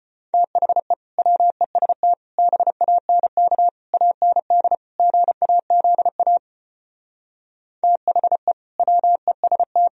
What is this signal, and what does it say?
Morse code, heard as a keyed tone, about 33 wpm, 700 Hz. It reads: THE WEST BANK AND GAZA THE WEST